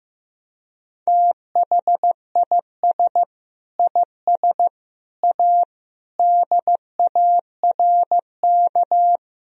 Morse code, heard as a keyed tone, about 14 wpm, 700 Hz. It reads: THIS IS A DARK
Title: Sentences from Top 1000 Words - Review Part 1 - 15wpm